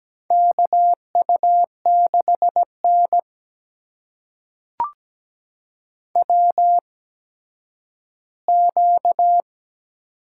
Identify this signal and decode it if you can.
Morse code, heard as a keyed tone, about 17 wpm, spaced as for 10 wpm, 700 Hz. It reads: KU6N WQ
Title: USA Call Signs - 4 Characters - 8x Character Spacing - 17wpm